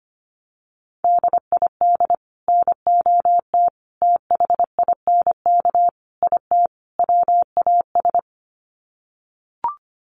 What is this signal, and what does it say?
Morse code, heard as a keyed tone, about 25 wpm, 700 Hz. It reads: DID NOT THINK IT WAS
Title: Sets of 5 Words Encore 25wpm